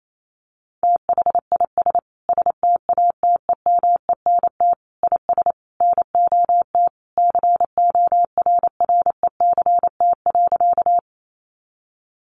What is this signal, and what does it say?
Morse code, heard as a keyed tone, about 28 wpm, 700 Hz. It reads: THIS STATEMENT IS NOT CORRECT.
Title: Sentences from Top 1200 Words 28wpm